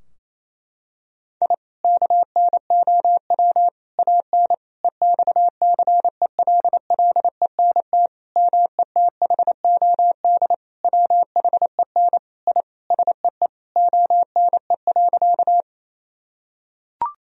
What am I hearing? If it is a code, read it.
Morse code, heard as a keyed tone, about 28 wpm, 700 Hz. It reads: I KNOW AN EXCELLENT METHOD WHEN I SEE ONE.
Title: Sentences from Top 1000 Words - Review Part 3 - 28wpm